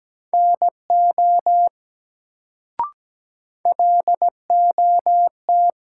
Morse code, heard as a keyed tone, about 17 wpm, 700 Hz. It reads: NO LOT
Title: Sets of 2 Words - New Word Drill - 17wpm